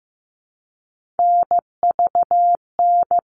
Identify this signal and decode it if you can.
Morse code, heard as a keyed tone, about 15 wpm, 700 Hz. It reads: NVN
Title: All Cut Numbers - 3-Digit 15wpm